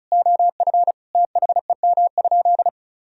Morse code, heard as a keyed tone, about 36 wpm, 700 Hz. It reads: OF THEM?
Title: Sentences from Top 1000 Words - Review Part 4 - 35wpm